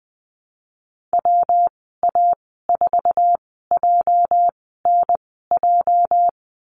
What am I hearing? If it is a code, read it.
Morse code, heard as a keyed tone, about 20 wpm, spaced as for 15 wpm, 700 Hz. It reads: WA4JNJ
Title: 213 Lesson 34g - Number 8 - Call Signs 20wpm